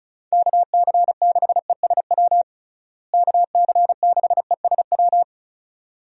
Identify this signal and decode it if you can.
Morse code, heard as a keyed tone, about 36 wpm, 700 Hz. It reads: KC6ESW KC6ESW
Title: USA Call Signs - 6 Characters - Repeated 2x - 35wpm